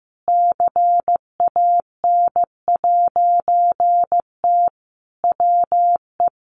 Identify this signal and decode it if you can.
Morse code, heard as a keyed tone, about 15 wpm, 700 Hz. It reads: CAN'T WE
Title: A Sound Of Thunder 15wpm